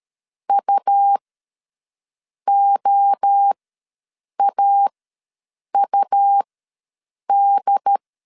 Morse code, heard as a keyed tone, about 13 wpm, spaced as for 6 wpm, 800 Hz. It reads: UOAUD
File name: LEZIONI / 8 YJ45 / 8 Y ws 12wpm 3409